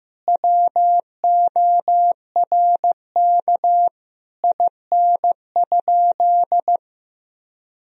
Morse code, heard as a keyed tone, about 15 wpm, 700 Hz. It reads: WORK IN?
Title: Sentences from Top 1000 Words - Review Part 4 - 15wpm